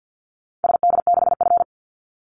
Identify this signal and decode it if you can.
Morse code, heard as a keyed tone, about 50 wpm, 700 Hz. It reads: SD6F
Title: Call Signs Repeated 2x - 50wpm